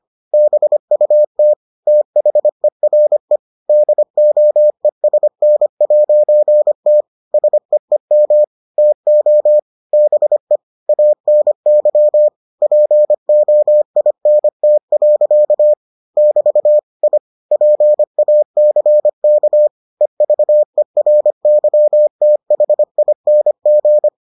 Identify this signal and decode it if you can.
Morse code, heard as a keyed tone, about 25 wpm, 600 Hz. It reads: BUT THERE DOESN'T SEEM TO BE ANY POINT. <BT> I PACK EVERYTHING